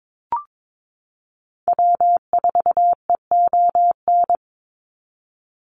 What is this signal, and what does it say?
Morse code, heard as a keyed tone, about 22 wpm, 700 Hz. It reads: W4EON
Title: NA SOTA Activator Callsigns 22wpm